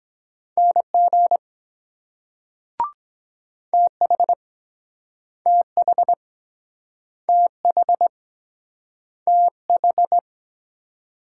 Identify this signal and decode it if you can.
Morse code, heard as a keyed tone, about 22 wpm, 700 Hz. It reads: NG TH TH TH TH